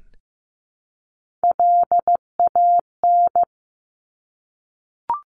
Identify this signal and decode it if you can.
Morse code, heard as a keyed tone, about 15 wpm, 700 Hz. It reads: LAN
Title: Top Three Letter Combinations in Words 15wpm